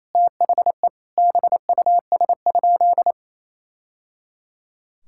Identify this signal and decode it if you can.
Morse code, heard as a keyed tone, about 28 wpm, 700 Hz. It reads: THE BUS?
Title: Sentences from Top 1000 Words - Review Part 6 - 28wpm